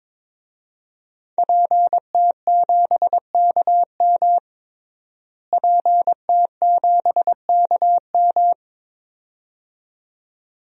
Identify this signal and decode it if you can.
Morse code, heard as a keyed tone, about 22 wpm, 700 Hz. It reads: PT7KM PT7KM
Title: DXCC-100 Call Signs - 5 Characters - Repeated 2x - 22wpm